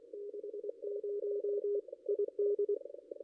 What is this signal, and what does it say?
Morse code, heard as a keyed tone, about 22 wpm, 440 Hz. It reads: 60 ID
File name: contests / ss-cw-k5tr-op-n5rz -0600